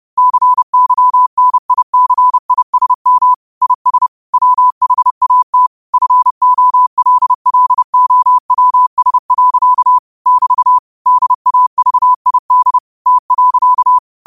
Morse code, heard as a keyed tone, about 30 wpm, 1000 Hz. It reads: CYNICISM IS WHAT FOLLOWS. <BT> DAVID T.
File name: qotdfiles / 30wpm